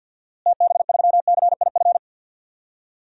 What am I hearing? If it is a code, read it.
Morse code, heard as a keyed tone, about 48 wpm, 700 Hz. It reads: TB3CIF
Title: DXCC-100 Call Signs - 6 Characters - Repeated 2x - 50wpm